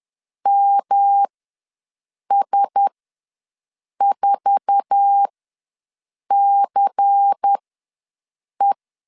Morse code, heard as a keyed tone, about 10 wpm, spaced as for 6 wpm, 800 Hz. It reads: MS4CE
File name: LEZIONI / 9 286 ws 10wpm 4032